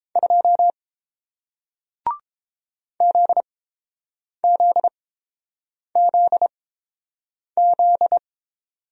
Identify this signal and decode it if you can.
Morse code, heard as a keyed tone, about 32 wpm, spaced as for 8 wpm, 700 Hz. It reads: 2 ZZZZ